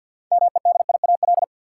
Morse code, heard as a keyed tone, about 49 wpm, 700 Hz. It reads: MEDIAL
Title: Practice QSO - Exchange 3 - More Info - 50wpm